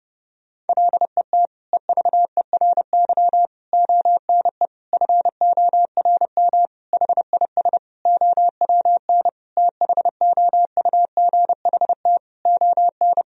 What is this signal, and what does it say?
Morse code, heard as a keyed tone, about 30 wpm, 700 Hz. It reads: LET EVERY ONE FORM HIS OWN THOUGHT ON
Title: Sentences from Top 1000 Words - Review Part 1 - 30wpm